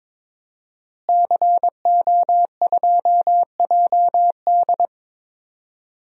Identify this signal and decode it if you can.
Morse code, heard as a keyed tone, about 22 wpm, 700 Hz. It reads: CO2JD
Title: DXCC-100 Call Signs - 5 Characters - Repeated 3x - 22wpm